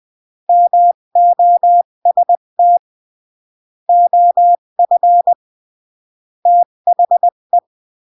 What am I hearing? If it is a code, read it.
Morse code, heard as a keyed tone, about 20 wpm, 700 Hz. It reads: MOST OF THE
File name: CW / jodorowsky / 82 commandments 20-15